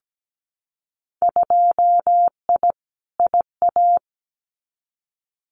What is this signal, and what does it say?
Morse code, heard as a keyed tone, about 17 wpm, 700 Hz. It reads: 2I IA
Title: Winter Field Day Exchanges Encore 17wpm